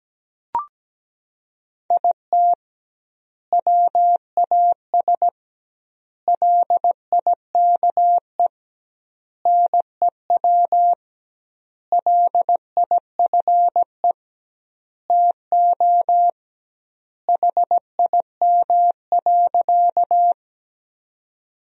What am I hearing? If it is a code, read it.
Morse code, heard as a keyed tone, about 17 wpm, 700 Hz. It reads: IT WAS LIKE NEW LIFE TO HIM.
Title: 2x Word-Spacing - Sentences from Top 300 Words - 17wpm